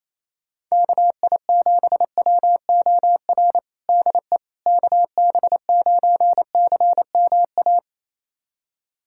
Morse code, heard as a keyed tone, about 28 wpm, 700 Hz. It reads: KI7WOR DE KB9CMA
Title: USA Callsign DE Callsign - 6 Characters - 28wpm